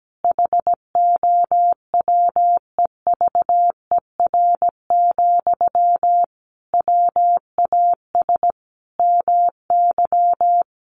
Morse code, heard as a keyed tone, about 17 wpm, 700 Hz. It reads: HOWEVER, WAS MY
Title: Sentences from Top 1500 Words 17wpm